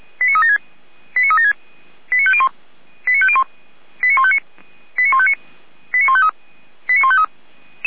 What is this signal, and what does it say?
Five-tone selective call (ZVEI), heard as quick runs of five tones, tones of about 70 ms. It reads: 89376 89501 89501 89159 89159 89153 89153